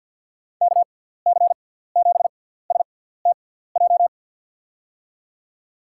Morse code, heard as a keyed tone, about 49 wpm, spaced as for 17 wpm, 700 Hz. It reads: KC7STJ
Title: USA Call Signs - 6x Character Spacing - 50wpm